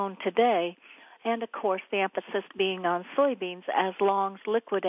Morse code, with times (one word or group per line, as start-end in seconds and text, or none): none